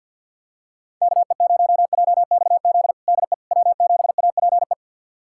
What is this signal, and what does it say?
Morse code, heard as a keyed tone, about 51 wpm, 700 Hz. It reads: KE0JXZ DE W7APE
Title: USA Callsign DE Callsign 50wpm